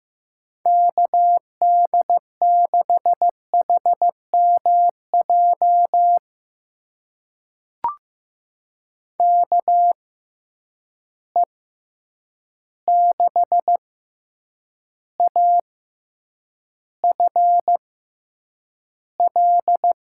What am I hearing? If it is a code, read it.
Morse code, heard as a keyed tone, about 15 wpm, spaced as for 8 wpm, 700 Hz. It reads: KD6HMJ KE6AFL
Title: USA Call Signs - 6 Characters - 6x Character Spacing - 15wpm